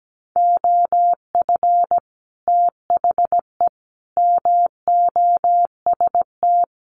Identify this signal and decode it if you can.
Morse code, heard as a keyed tone, about 17 wpm, 700 Hz. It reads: OF THE MOST
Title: Sets of 4 Words 17wpm